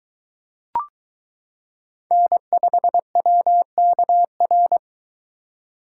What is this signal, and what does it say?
Morse code, heard as a keyed tone, about 22 wpm, 700 Hz. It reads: N5WKR